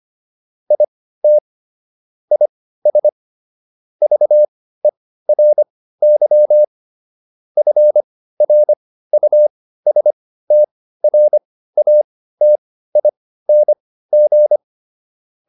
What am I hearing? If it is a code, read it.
Morse code, heard as a keyed tone, about 25 wpm, spaced as for 15 wpm, 600 Hz. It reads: IT IS VERY FRUSTRATING